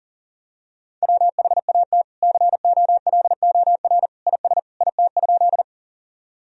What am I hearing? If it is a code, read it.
Morse code, heard as a keyed tone, about 40 wpm, 700 Hz. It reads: WHAT COLOR IS IT?
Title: Sentences from Top 1000 Words - Review Part 2 - 40wpm